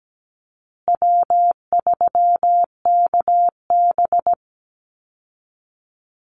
Morse code, heard as a keyed tone, about 17 wpm, 700 Hz. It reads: W3KB
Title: Easy Call Signs Repeated 3x - 17wpm